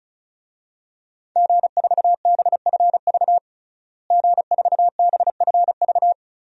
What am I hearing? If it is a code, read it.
Morse code, heard as a keyed tone, about 36 wpm, 700 Hz. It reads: G4BFV G4BFV
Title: DXCC-100 Call Signs - 5 Characters - Repeated 2x - 35wpm